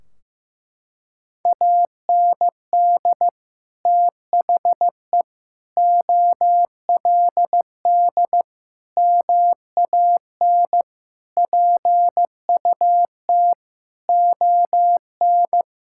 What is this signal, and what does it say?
Morse code, heard as a keyed tone, about 16 wpm, 700 Hz. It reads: AND THE OLD MAN PUT ON
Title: Sentences from Top 1600 Words 15wpm